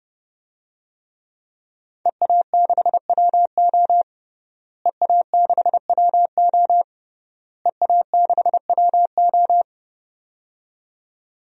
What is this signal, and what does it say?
Morse code, heard as a keyed tone, about 30 wpm, 700 Hz. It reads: EA6WO EA6WO EA6WO
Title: DXCC-100 Call Signs - 5 Characters - Repeated 3x - 30wpm